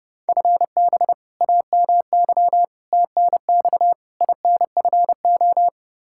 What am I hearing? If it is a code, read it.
Morse code, heard as a keyed tone, about 30 wpm, 700 Hz. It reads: FB AMY TNX INFO
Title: Practice QSO - Exchange 3 - More Info - 30wpm